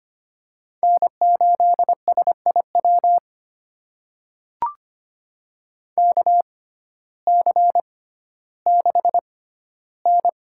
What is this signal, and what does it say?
Morse code, heard as a keyed tone, about 25 wpm, spaced as for 14 wpm, 700 Hz. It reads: N8SIW KC6N
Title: USA Call Signs - 6x Character Spacing - 25wpm